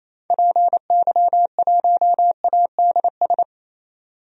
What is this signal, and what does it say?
Morse code, heard as a keyed tone, about 28 wpm, 700 Hz. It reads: PY1ADS
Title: DXCC-100 Call Signs - 6 Characters - 28wpm